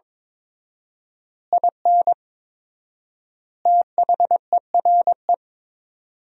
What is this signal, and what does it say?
Morse code, heard as a keyed tone, about 23 wpm, 700 Hz. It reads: IN THERE
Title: 4x Word-Spacing - Sentences from Top 500 Words - 22wpm